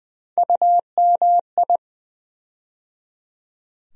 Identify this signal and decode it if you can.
Morse code, heard as a keyed tone, about 20 wpm, 700 Hz. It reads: UMI
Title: USA Call Sign Postfixes - 3 Letters 20wpm